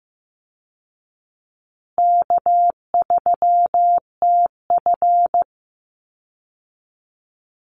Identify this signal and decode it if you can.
Morse code, heard as a keyed tone, about 14 wpm, 700 Hz. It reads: K3TF
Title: Call Signs Repeated 2x - 15wpm